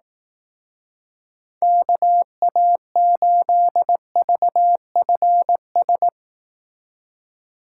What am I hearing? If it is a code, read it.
Morse code, heard as a keyed tone, about 18 wpm, 700 Hz. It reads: KA8VFS
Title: Speed Racing - USA 2x3 Call Signs 15wpm